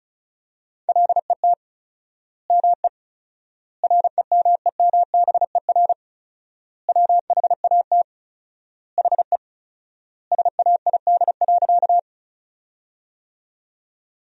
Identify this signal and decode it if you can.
Morse code, heard as a keyed tone, about 34 wpm, 700 Hz. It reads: LET ME REMEMBER WHAT HE SAID.
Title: 4x Word-Spacing - Sentences from Top 400 Words - 35wpm